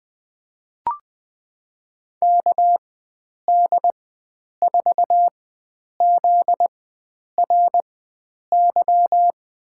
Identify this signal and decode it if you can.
Morse code, heard as a keyed tone, about 20 wpm, spaced as for 9 wpm, 700 Hz. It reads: KD4ZRY
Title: USA Call Signs - 6 Characters - 4x Character Spacing - 20wpm